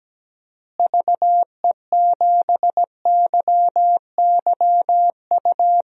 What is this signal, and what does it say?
Morse code, heard as a keyed tone, about 17 wpm, 700 Hz. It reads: VE7YYU
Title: DXCC-100 Call Signs - 6 Characters - Repeated 2x - 17wpm